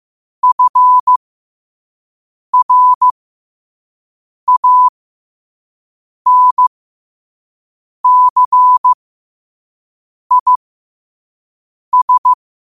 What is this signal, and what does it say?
Morse code, heard as a keyed tone, about 15 wpm, spaced as for 5 wpm, 1000 Hz. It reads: FRANCIS
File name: qotdfiles / 5wpm